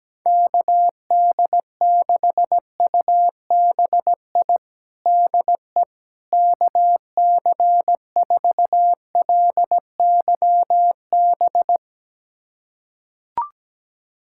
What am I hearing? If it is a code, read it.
Morse code, heard as a keyed tone, about 17 wpm, 700 Hz. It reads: KD6UBI DE KC4LYB
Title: USA Callsign DE Callsign - 6 Characters - 17wpm